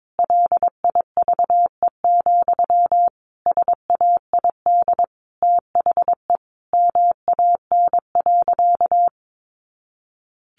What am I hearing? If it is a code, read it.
Morse code, heard as a keyed tone, about 22 wpm, 700 Hz. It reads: LIVE, SAID THE MAN.
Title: Sentences from Top 200 Words 22wpm